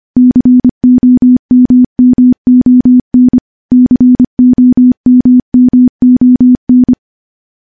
Morse code, heard as a keyed tone, about 25 wpm, 250 Hz. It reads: COMMON COMMON
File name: morse / morse-training-25WPM